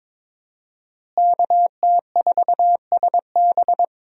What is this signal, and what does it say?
Morse code, heard as a keyed tone, about 22 wpm, 700 Hz. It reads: KT4SB
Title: USA Call Signs - 5 Characters - Repeated 3x - 22wpm